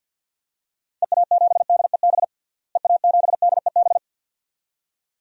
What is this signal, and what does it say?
Morse code, heard as a keyed tone, about 47 wpm, 700 Hz. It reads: EA7DEB EA7DEB
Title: DXCC-100 Call Signs - 6 Characters - Repeated 2x - 50wpm